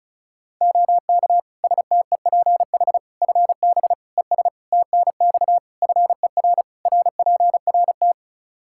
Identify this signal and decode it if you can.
Morse code, heard as a keyed tone, about 36 wpm, 700 Hz. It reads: OK STEPH FB ES TNX FER RPRT
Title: Practice QSO - Exchange 2 - Info - 35wpm